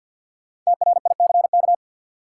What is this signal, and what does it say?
Morse code, heard as a keyed tone, about 53 wpm, 700 Hz. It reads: TRICK
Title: Sentences from Top 1500 Words - New Word Drill - 50wpm